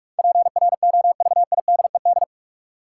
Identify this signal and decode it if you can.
Morse code, heard as a keyed tone, about 46 wpm, 700 Hz. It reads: PROVIDED
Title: Sentences from Top 1500 Words - New Word Drill Alt - 45wpm